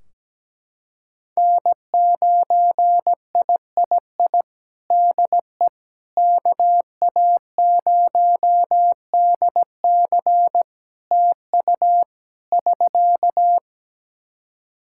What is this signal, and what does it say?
Morse code, heard as a keyed tone, about 17 wpm, 700 Hz. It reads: N9III DE KA0DC TU <SK>
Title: Practice QSO - Exchange 4 - Ending - 17wpm